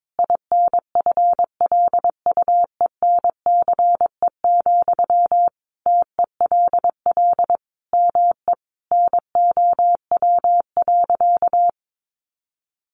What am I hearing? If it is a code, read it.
Morse code, heard as a keyed tone, about 22 wpm, 700 Hz. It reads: INFLUENCE, TELL ME NOW.
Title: Sentences from Top 1100 Words 22wpm